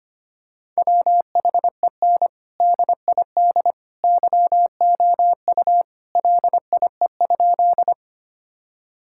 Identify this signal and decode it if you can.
Morse code, heard as a keyed tone, about 25 wpm, 700 Hz. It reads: WHEN DID YOU LIE?
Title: Sentences from Top 1600 Words 25wpm